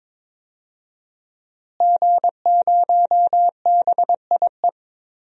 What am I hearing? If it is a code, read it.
Morse code, heard as a keyed tone, about 22 wpm, 700 Hz. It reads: G0BIE
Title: DXCC-100 Call Signs - 5 Characters - Repeated 3x - 22wpm